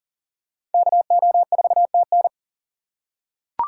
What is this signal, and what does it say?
Morse code, heard as a keyed tone, about 39 wpm, 700 Hz. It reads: KO4TN
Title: USA Call Signs - 5 Characters - 6x Character Spacing - 40wpm